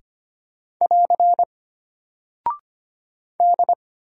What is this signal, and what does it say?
Morse code, heard as a keyed tone, about 25 wpm, 700 Hz. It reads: <AR> D
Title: Single Character (Letter:Number:Punctuation:Prosign) Alt 25wpm